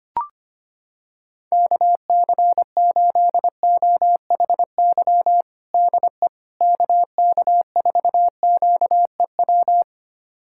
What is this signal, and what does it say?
Morse code, heard as a keyed tone, about 25 wpm, 700 Hz. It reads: KC8OHY DE KK4QEW
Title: USA Callsign DE Callsign 25wpm